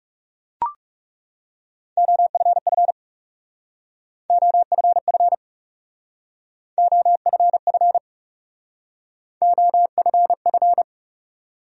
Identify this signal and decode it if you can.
Morse code, heard as a keyed tone, about 37 wpm, 700 Hz. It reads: OFF OFF OFF OFF
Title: Speed-Racing Top 300 Words 30wpm